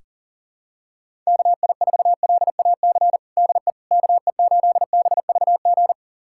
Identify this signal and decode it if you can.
Morse code, heard as a keyed tone, about 40 wpm, 700 Hz. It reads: KI4LAC DE KE8BVG
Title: USA Callsign DE Callsign - 6 Characters - 40wpm